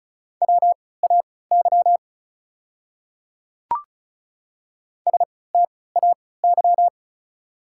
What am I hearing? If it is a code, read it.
Morse code, heard as a keyed tone, about 34 wpm, spaced as for 20 wpm, 690 Hz. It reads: WAY STAY
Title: 3x Character Spacing - Top 500 Words - 35wpm